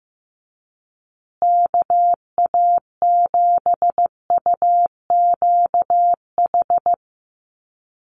Morse code, heard as a keyed tone, about 15 wpm, 700 Hz. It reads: KA7UQH
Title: USA Call Signs Repeated 2x 15wpm